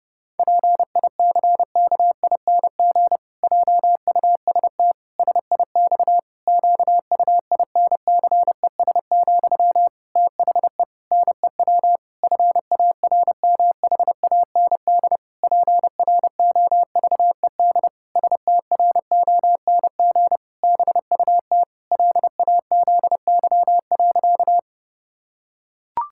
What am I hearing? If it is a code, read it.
Morse code, heard as a keyed tone, about 30 wpm, 700 Hz. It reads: PICKING JUST SIX QUINCES, THE NEW FARMHAND PROVED STRONG BUT LAZY.
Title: Pangram Challenge v2 - 30wpm